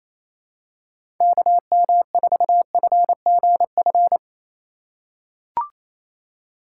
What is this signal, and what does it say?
Morse code, heard as a keyed tone, about 28 wpm, 700 Hz. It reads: KM4FGF
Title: USA Call Signs - 6 Characters - 4x Character Spacing - 28wpm